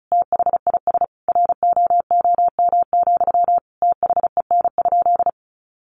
Morse code, heard as a keyed tone, about 34 wpm, 700 Hz. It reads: THIS ROOM, THEN?
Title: Sentences from Top 600 Words 35wpm